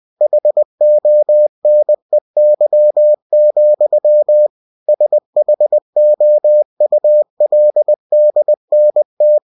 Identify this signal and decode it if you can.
Morse code, heard as a keyed tone, about 20 wpm, 600 Hz. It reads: HONEY, SHOULDNT
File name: cw-books / winnie-the-pooh / winnie-20wpm-page-0009